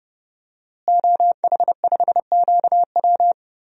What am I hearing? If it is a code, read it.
Morse code, heard as a keyed tone, about 32 wpm, 700 Hz. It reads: OH5QW
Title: DXCC-100 Call Signs - 5 Characters - Repeated 2x - 30wpm